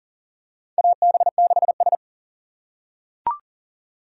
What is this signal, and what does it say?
Morse code, heard as a keyed tone, about 37 wpm, 700 Hz. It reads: AB6S
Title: USA Call Signs - 4 Characters - 2x Character Spacing - 40wpm